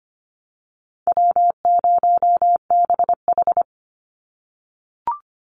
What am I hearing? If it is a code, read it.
Morse code, heard as a keyed tone, about 25 wpm, 700 Hz. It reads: W0BH
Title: Easy Call Signs - 25wpm